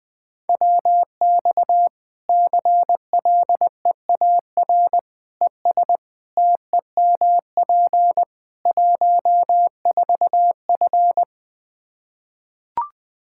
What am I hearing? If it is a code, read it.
Morse code, heard as a keyed tone, about 20 wpm, 700 Hz. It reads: WX CLEAR ES TEMP 14F
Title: K1USN SST QSO Practice 20wpm